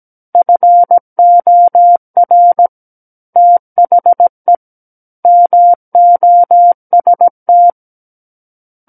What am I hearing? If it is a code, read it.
Morse code, heard as a keyed tone, about 17 wpm, 700 Hz. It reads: FOR THE MOST
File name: Book One / 08-Chapter 8 - 17 WPM